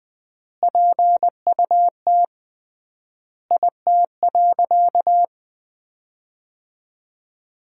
Morse code, heard as a keyed tone, about 21 wpm, 700 Hz. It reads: PUT IT.
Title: 3x Word-Spacing - Sentences from Top 200 Words - 20wpm